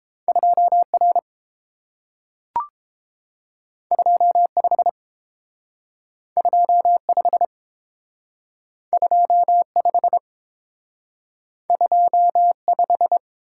Morse code, heard as a keyed tone, about 32 wpm, 700 Hz. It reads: 2R 25 25 25 25